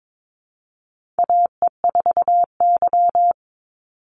Morse code, heard as a keyed tone, about 22 wpm, 700 Hz. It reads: AE4Y
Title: NA SOTA Chaser Callsigns 22wpm